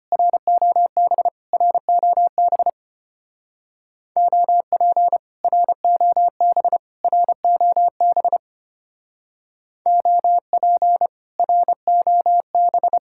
Morse code, heard as a keyed tone, about 30 wpm, 700 Hz. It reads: ROB ROB OP ROB ROB OP ROB
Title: Speed-Racing Common QSO Phrases Encore - 25wpm